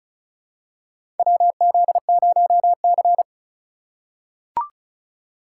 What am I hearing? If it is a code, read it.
Morse code, heard as a keyed tone, about 34 wpm, 700 Hz. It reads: WZ0C
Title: USA Call Signs - 4 Characters - 3x Character Spacing - 35wpm